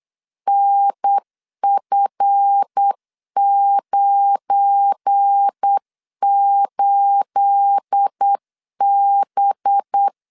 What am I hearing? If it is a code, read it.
Morse code, heard as a keyed tone, about 8 wpm, 800 Hz. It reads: NF98B
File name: LEZIONI / 9 28 8wpm 3512